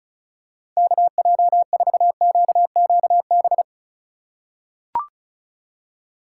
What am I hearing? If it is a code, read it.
Morse code, heard as a keyed tone, about 34 wpm, 700 Hz. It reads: KJ4QQB
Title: USA Call Signs - 2x Character Spacing - 35wpm